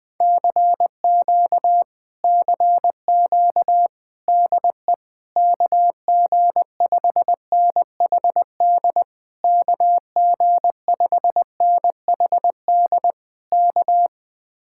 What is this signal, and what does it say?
Morse code, heard as a keyed tone, about 20 wpm, 700 Hz. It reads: CQ CQ DE KG5NHD KG5NHD K
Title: Speed-Racing Common QSO Phrases - 20wpm